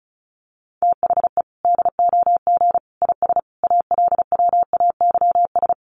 Morse code, heard as a keyed tone, about 34 wpm, 700 Hz. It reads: THE DOG IS ALWAYS
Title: Sentences from Top 400 Words 35wpm